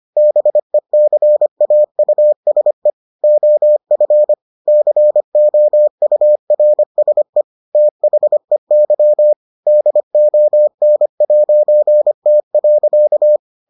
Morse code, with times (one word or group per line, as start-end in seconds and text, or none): 0.2-2.9 BECAUSE
3.2-4.3 OF
4.7-7.4 COURSE
7.7-9.3 THEY
9.7-13.4 DON'T.